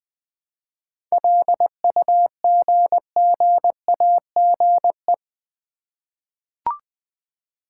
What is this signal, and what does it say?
Morse code, heard as a keyed tone, about 20 wpm, 700 Hz. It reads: LUGGAGE
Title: Easy Words with Hints - 20wpm